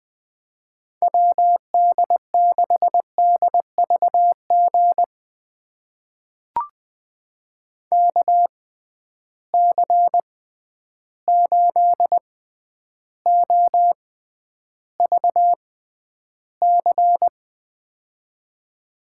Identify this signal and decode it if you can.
Morse code, heard as a keyed tone, about 20 wpm, spaced as for 10 wpm, 700 Hz. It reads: WD6DVG KC8OVC
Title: USA Call Signs - 6 Characters - 6x Character Spacing - 20wpm